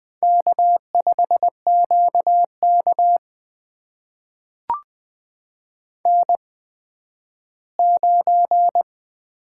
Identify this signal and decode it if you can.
Morse code, heard as a keyed tone, about 20 wpm, spaced as for 12 wpm, 700 Hz. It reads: K5QK N9
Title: USA Call Signs - 8x Character Spacing - 20wpm